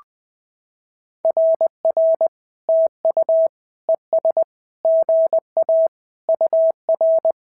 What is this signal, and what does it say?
Morse code, heard as a keyed tone, about 19 wpm, 650 Hz. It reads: RR TU ES GA UR
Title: SOTA QSO Practice 20wpm